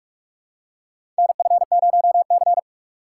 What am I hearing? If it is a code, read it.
Morse code, heard as a keyed tone, about 46 wpm, 700 Hz. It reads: NF0C
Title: USA Call Signs - 4 Characters - 3x Character Spacing - 45wpm